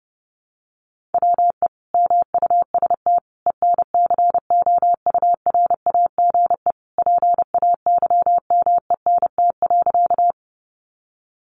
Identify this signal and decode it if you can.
Morse code, heard as a keyed tone, about 30 wpm, 700 Hz. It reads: WE MUST ENCOURAGE PAYMENT.
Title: Sentences from Top 1400 Words 30wpm